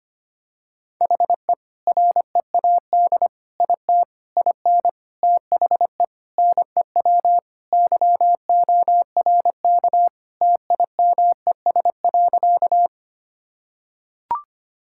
Morse code, heard as a keyed tone, about 25 wpm, 700 Hz. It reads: HE READ IT IN THE NEW YORK TIMES.
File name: practice / 25wpm / Sentences with 4-Word Phrases - 25wpm